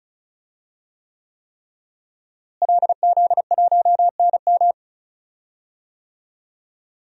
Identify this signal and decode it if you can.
Morse code, heard as a keyed tone, about 34 wpm, 700 Hz. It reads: LZ1NM
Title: DXCC-100 Call Signs - 5 Characters - Repeated 3x - 35wpm